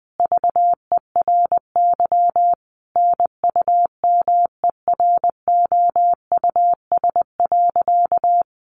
Morse code, heard as a keyed tone, about 20 wpm, 700 Hz. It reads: VERY NUMEROUS.
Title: Sentences from Top 1400 Words 20wpm